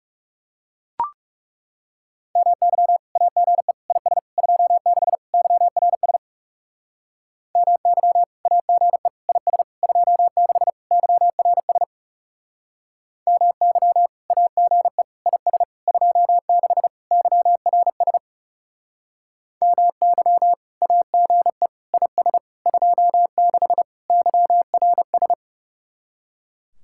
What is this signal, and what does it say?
Morse code, heard as a keyed tone, about 37 wpm, 700 Hz. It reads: MY AGE IS 26 YRS MY AGE IS 26 YRS MY AGE IS 26 YRS MY AGE IS 26 YRS
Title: Speed-Racing Common QSO Phrases - 30wpm